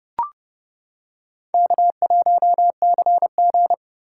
Morse code, heard as a keyed tone, about 28 wpm, 700 Hz. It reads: K1CG